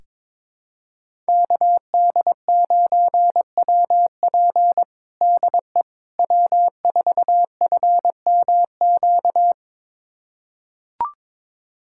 Morse code, heard as a keyed tone, about 22 wpm, 700 Hz. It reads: KD9WP DE W4FMQ
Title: USA Callsign DE Callsign - 5 Characters - 22wpm